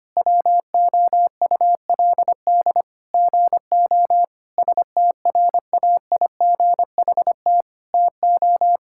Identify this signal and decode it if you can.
Morse code, heard as a keyed tone, about 25 wpm, 700 Hz. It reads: WOULD GO STRAIGHT TO
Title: Sentences from Top 1000 Words - Review Part 6 - 25wpm